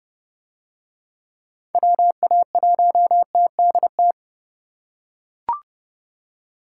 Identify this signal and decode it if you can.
Morse code, heard as a keyed tone, about 30 wpm, 700 Hz. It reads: WA1TDT